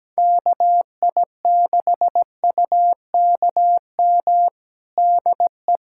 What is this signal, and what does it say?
Morse code, heard as a keyed tone, about 17 wpm, 700 Hz. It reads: KI6UKM DE
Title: USA Callsign DE Callsign - 6 Characters - 17wpm